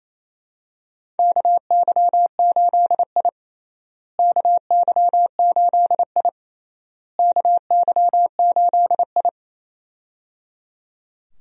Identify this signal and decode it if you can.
Morse code, heard as a keyed tone, about 27 wpm, 700 Hz. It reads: KY8I KY8I KY8I
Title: USA Call Signs - 4 Characters - Repeated 3x - 28wpm